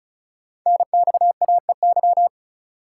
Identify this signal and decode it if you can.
Morse code, heard as a keyed tone, about 36 wpm, 700 Hz. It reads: NXAEY
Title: Code Groups - 5 Letters - 35wpm